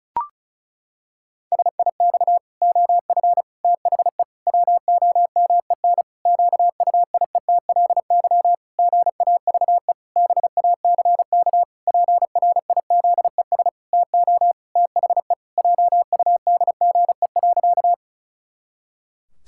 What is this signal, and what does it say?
Morse code, heard as a keyed tone, about 36 wpm, 700 Hz. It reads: SIX OF THE WOMEN QUIETLY GAVE BACK PRIZES TO THE JUDGE.
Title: Pangram Challenge v2 - 35wpm